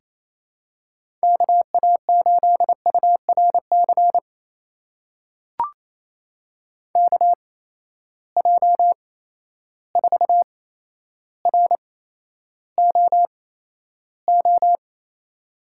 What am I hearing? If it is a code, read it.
Morse code, heard as a keyed tone, about 28 wpm, spaced as for 12 wpm, 700 Hz. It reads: KA8URC KJ4ROO
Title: USA Call Signs - 6 Characters - 8x Character Spacing - 28wpm